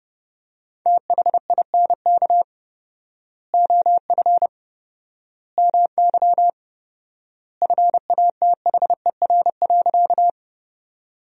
Morse code, heard as a keyed tone, about 30 wpm, 700 Hz. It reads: THINK OF MY FATHER.
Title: 4x Word-Spacing - Sentences from Top 300 Words - 30wpm